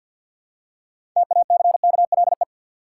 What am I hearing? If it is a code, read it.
Morse code, heard as a keyed tone, about 50 wpm, 700 Hz. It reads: TACKLE
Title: Six Letter Words - 50wpm